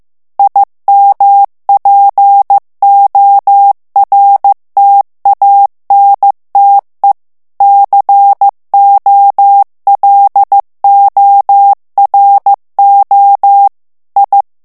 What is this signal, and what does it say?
Morse code, heard as a keyed tone, about 15 wpm, 800 Hz. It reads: IMPORTANTE COLORO I
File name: morse mp3 / ITALIEN texte clair 2 vitesse 14 mots minute